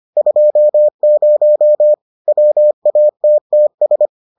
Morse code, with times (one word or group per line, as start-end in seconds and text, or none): 0.2-1.9 20
2.3-4.1 WATTS